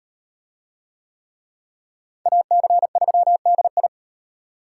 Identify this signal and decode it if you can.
Morse code, heard as a keyed tone, about 38 wpm, 700 Hz. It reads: AC3DI